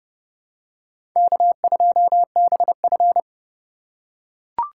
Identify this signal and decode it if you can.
Morse code, heard as a keyed tone, about 30 wpm, 700 Hz. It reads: K2BF
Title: USA Call Signs - 4 Characters - 4x Character Spacing - 30wpm